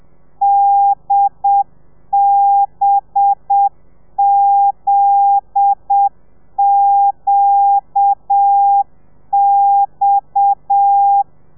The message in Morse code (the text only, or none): DBZQX